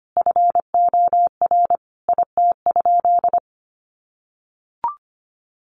Morse code, heard as a keyed tone, about 25 wpm, 700 Hz. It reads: FOR IT?
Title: Sentences from Top 400 Words 25wpm